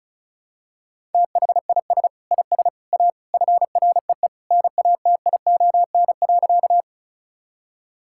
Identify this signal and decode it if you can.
Morse code, heard as a keyed tone, about 36 wpm, 700 Hz. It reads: THIS IS A FREE NATION.
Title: Sentences from Top 1000 Words - Review Part 6 - 35wpm